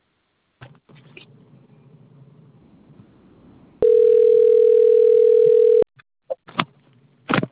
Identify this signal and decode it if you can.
Ringback tone (North America), 440 + 480 Hz.